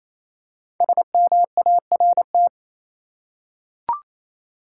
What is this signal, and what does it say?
Morse code, heard as a keyed tone, about 28 wpm, 700 Hz. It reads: SMART
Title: Five Letter Words - 28wpm